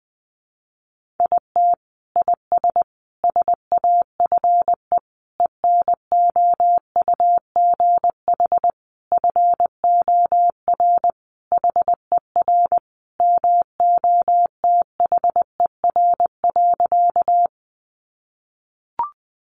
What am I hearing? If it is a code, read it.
Morse code, heard as a keyed tone, about 20 wpm, 700 Hz. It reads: IT IS SAFE ENOUGH FOR HER MOTHER.
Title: Sentences from Top 900 Words 20wpm